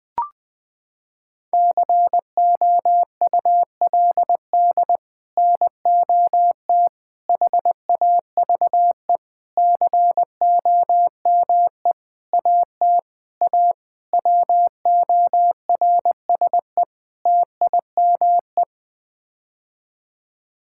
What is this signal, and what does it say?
Morse code, heard as a keyed tone, about 20 wpm, 700 Hz. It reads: COULD NOT HAVE COME AT A WORSE TIME
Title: Sets of 8 Words - 20wpm